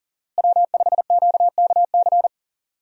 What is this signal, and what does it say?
Morse code, heard as a keyed tone, about 37 wpm, 700 Hz. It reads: W5QKC
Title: USA Call Signs - 5 Characters - Repeated 3x - 40wpm